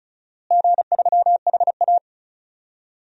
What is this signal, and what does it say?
Morse code, heard as a keyed tone, about 36 wpm, 700 Hz. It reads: G3HA
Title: DXCC-100 Call Signs - 4 Characters - Repeated 3x - 35wpm